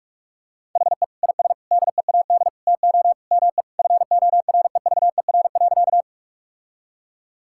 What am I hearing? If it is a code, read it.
Morse code, heard as a keyed tone, about 47 wpm, 700 Hz. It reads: HE IS DEAD TO ME FOREVER.
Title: Sentences from Top 1600 Words 45wpm